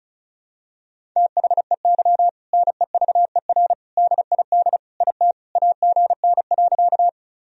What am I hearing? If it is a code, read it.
Morse code, heard as a keyed tone, about 36 wpm, 700 Hz. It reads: THEY NEVER DID IT AGN.
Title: Sentences from Top 300 Words and Abbreviations 35wpm